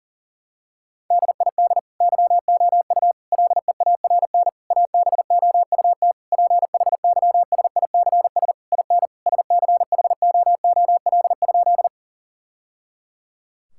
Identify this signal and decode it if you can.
Morse code, heard as a keyed tone, about 40 wpm, 700 Hz. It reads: DID YOU LEARN ABOUT PHYSICS IN SCHOOL?
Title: Sentences from Top 1000 Words - Review Part 5 - 40wpm